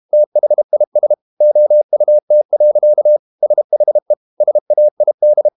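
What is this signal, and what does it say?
Morse code, heard as a keyed tone, about 32 wpm, 600 Hz. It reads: THIS OUT. SHE SAID
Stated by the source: Morse code practice at 32 wpm from CW med Gnister, reading Ken Follett's novel Never in English.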